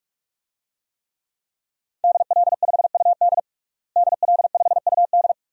Morse code, heard as a keyed tone, about 46 wpm, 700 Hz. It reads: DL5UD DL5UD
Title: DXCC-100 Call Signs - 5 Characters - Repeated 2x - 45wpm